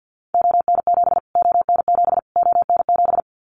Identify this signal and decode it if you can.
Morse code, heard as a keyed tone, about 50 wpm, 700 Hz. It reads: ON7 ON7 ON7
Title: Callsign Prefixes for Top 25 DXCC - Repeated 3x - 50wpm